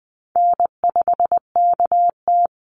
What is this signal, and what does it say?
Morse code, heard as a keyed tone, about 20 wpm, 700 Hz. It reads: N5KT
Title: Easy Call Signs - 20wpm